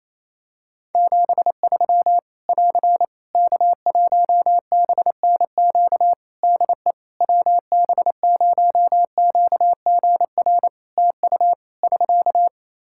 Morse code, heard as a keyed tone, about 28 wpm, 700 Hz. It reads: 73 <AR> K1BNQ DE WB0QGR TU <SK>
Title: Practice QSO - Exchange 4 - Ending - 28wpm